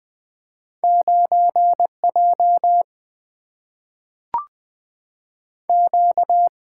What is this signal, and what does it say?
Morse code, heard as a keyed tone, about 20 wpm, 700 Hz. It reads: 9 J Q
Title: Single-Letter Single-Number 20wpm